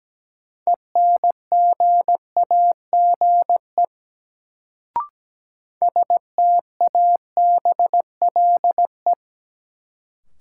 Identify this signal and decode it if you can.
Morse code, heard as a keyed tone, about 17 wpm, 700 Hz. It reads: ENGAGE STABLE
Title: Sentences from Top 1500 Words - New Word Drill Alt - 17wpm